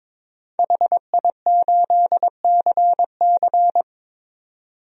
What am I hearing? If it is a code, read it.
Morse code, heard as a keyed tone, about 22 wpm, 700 Hz. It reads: HI8CC
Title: DXCC-100 Call Signs - 5 Characters - 22wpm